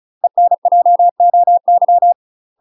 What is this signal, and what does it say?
Morse code, heard as a keyed tone, about 34 wpm, 700 Hz. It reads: ENJOY
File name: UK MASTER QSOs TRAINING / 35WPM / UK MASTER QSOs 0034